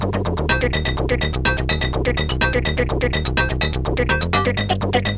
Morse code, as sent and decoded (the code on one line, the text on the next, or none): none